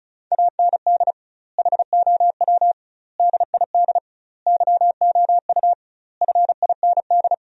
Morse code, heard as a keyed tone, about 36 wpm, 700 Hz. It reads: AND HOW DID YOU FIND
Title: 2x Word-Spacing - Sentences from Top 100 Words - 35wpm